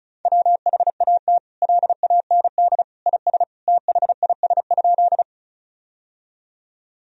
Morse code, heard as a keyed tone, about 36 wpm, 700 Hz. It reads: WHAT LAND IS THIS?
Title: Sentences from Top 1000 Words - Review Part 1 - 35wpm